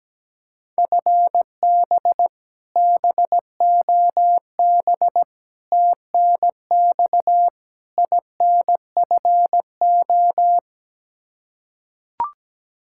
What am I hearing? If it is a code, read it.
Morse code, heard as a keyed tone, about 17 wpm, 700 Hz. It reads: FB BOB TNX INFO
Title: K1USN SST QSO Practice 17wpm